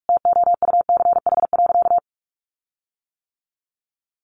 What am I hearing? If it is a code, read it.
Morse code, heard as a keyed tone, about 44 wpm, 700 Hz. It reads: TOUCH.
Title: Sentences from Top 1500 Words 45wpm